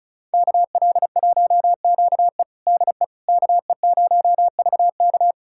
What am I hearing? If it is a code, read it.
Morse code, heard as a keyed tone, about 36 wpm, 690 Hz. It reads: KL1QE DE KE0VK
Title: USA Callsign DE Callsign - 5 Characters - 35wpm